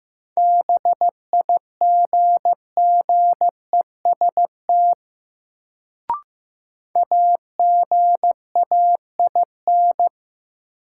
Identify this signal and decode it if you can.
Morse code, heard as a keyed tone, about 15 wpm, 700 Hz. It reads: BIGGEST AGAIN
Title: Sets of 4 Words - New Word Drill - 15wpm